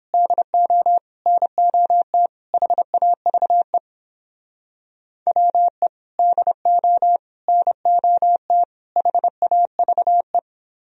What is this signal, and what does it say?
Morse code, heard as a keyed tone, about 29 wpm, 700 Hz. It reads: DO NOT HAVE WE DO NOT HAVE